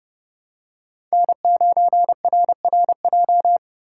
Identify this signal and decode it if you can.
Morse code, heard as a keyed tone, about 30 wpm, 700 Hz. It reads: N9RRJ
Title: USA Call Signs - 5 Characters - 2x Character Spacing - 30wpm